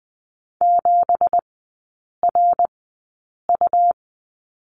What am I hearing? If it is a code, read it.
Morse code, heard as a keyed tone, about 20 wpm, spaced as for 8 wpm, 700 Hz. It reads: 7RU
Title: 205 Lesson 33e - Letter j - Three Characters 20wpm